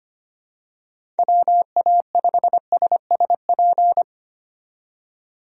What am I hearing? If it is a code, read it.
Morse code, heard as a keyed tone, about 25 wpm, 700 Hz. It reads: WA5SSP
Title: USA Call Signs - 6 Characters - 25wpm